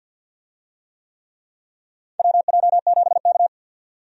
Morse code, heard as a keyed tone, about 52 wpm, 700 Hz. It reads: WJ7K
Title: USA Call Signs - 4 Characters - 50wpm